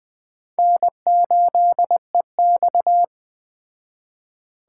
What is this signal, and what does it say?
Morse code, heard as a keyed tone, about 21 wpm, 700 Hz. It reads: N8EX
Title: Speed Racing - USA 1x2 Call Signs 17wpm